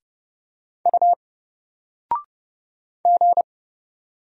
Morse code, heard as a keyed tone, about 31 wpm, 700 Hz. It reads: U G
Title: Single Letter-Number 30wpm